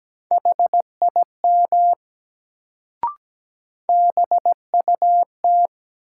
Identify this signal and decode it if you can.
Morse code, heard as a keyed tone, about 17 wpm, 700 Hz. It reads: HIM BUT
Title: Sets of 2 Words - New Word Drill - 17wpm